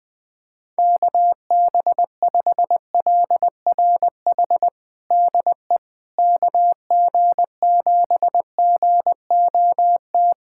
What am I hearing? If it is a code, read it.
Morse code, heard as a keyed tone, about 20 wpm, 700 Hz. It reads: KB5LRH DE KG7GOT
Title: USA Callsign DE Callsign 20wpm